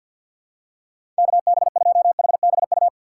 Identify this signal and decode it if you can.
Morse code, heard as a keyed tone, about 49 wpm, 700 Hz. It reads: KB2HBU
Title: USA Call Signs - 6 Characters - 4x Character Spacing - 50wpm